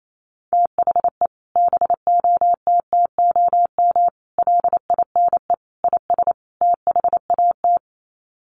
Morse code, heard as a keyed tone, about 28 wpm, 700 Hz. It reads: THE BOTTOM LINE IS THAT
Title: Sets of 5 Words Encore 28wpm